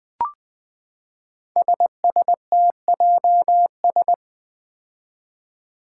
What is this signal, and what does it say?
Morse code, heard as a keyed tone, about 20 wpm, 700 Hz. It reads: SSTJS
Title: Five Challenging Letter-Number Combinations 20wpm